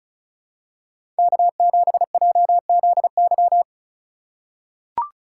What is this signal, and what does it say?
Morse code, heard as a keyed tone, about 36 wpm, 700 Hz. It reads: K7JZY
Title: USA Call Signs - 3x Character Spacing - 35wpm